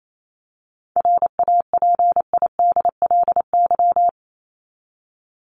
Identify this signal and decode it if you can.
Morse code, heard as a keyed tone, about 28 wpm, 700 Hz. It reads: RAPIDLY
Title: LY Suffix - Top 100 Words with Base Words 28wpm